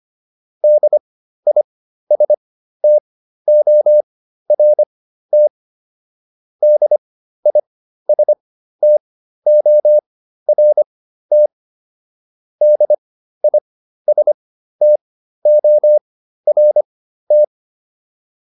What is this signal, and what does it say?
Morse code, heard as a keyed tone, about 25 wpm, spaced as for 13 wpm, 600 Hz. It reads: DISTORT DISTORT DISTORT